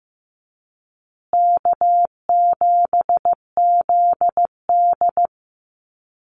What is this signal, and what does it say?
Morse code, heard as a keyed tone, about 14 wpm, 700 Hz. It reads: K7ZD
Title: Easy Call Signs Repeated 3x - 15wpm